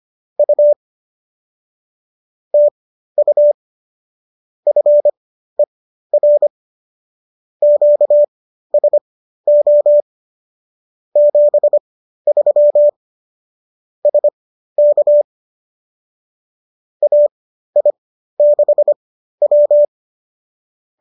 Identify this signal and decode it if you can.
Morse code, heard as a keyed tone, about 25 wpm, spaced as for 13 wpm, 590 Hz. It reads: U TU FER QSO 73 SK AI6W
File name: qso207 13